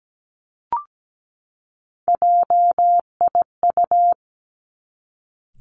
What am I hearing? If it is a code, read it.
Morse code, heard as a keyed tone, about 17 wpm, 700 Hz. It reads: JIU
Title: Three Challenging Letter-Number Combinations 17wpm